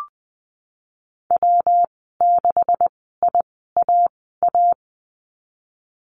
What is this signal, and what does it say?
Morse code, heard as a keyed tone, about 20 wpm, spaced as for 15 wpm, 700 Hz. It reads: W6IAA
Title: 167 Lesson 27g - Number 7 - Call Signs 20wpm